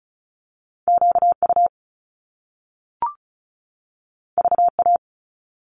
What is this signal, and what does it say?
Morse code, heard as a keyed tone, about 36 wpm, 700 Hz. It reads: QU VA
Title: Top Two Letter Combinations in Words 35wpm